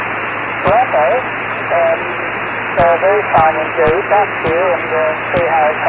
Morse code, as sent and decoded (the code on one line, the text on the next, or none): none